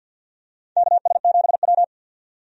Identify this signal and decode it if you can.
Morse code, heard as a keyed tone, about 51 wpm, 700 Hz. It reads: KS7W
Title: USA Call Signs - 4 Characters - 50wpm